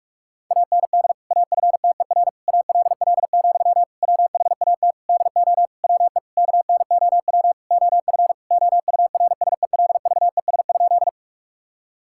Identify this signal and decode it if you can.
Morse code, heard as a keyed tone, about 46 wpm, 700 Hz. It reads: AND AFTER ALL, WHAT DO WE KNOW OF OURSELVES?
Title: Sentences from Top 1600 Words 45wpm